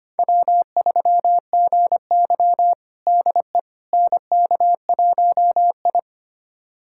Morse code, heard as a keyed tone, about 25 wpm, 700 Hz. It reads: W3GY DE NK1I
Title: USA Callsign DE Callsign - 4 Characters - 25wpm